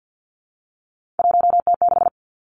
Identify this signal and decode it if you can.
Morse code, heard as a keyed tone, about 50 wpm, 700 Hz. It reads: 1T6
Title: Three Challenging Letter-Number Combinations 50wpm